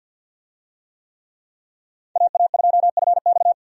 Morse code, heard as a keyed tone, about 47 wpm, 700 Hz. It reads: AA2FX
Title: USA Call Signs - 5 Characters - 50wpm